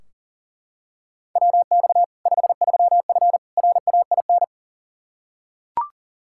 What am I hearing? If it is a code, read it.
Morse code, heard as a keyed tone, about 40 wpm, 700 Hz. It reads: WX 53F RAIN
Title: Practice QSO - Exchange 2 - Info - 40wpm